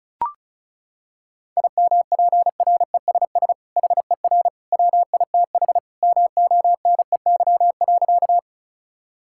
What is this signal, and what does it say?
Morse code, heard as a keyed tone, about 36 wpm, 700 Hz. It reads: IMPRESS HER WITH MONEY.
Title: Sentences from Top 1600 Words 35wpm